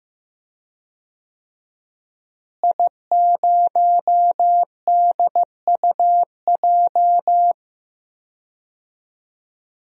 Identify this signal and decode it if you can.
Morse code, heard as a keyed tone, about 15 wpm, 700 Hz. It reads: I0DUJ
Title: DXCC-100 Call Signs - 5 Characters - Repeated 2x - 15wpm